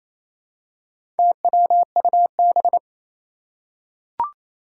Morse code, heard as a keyed tone, about 27 wpm, 700 Hz. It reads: TWUB
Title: Four Challenging Letter-Number Combinations 28wpm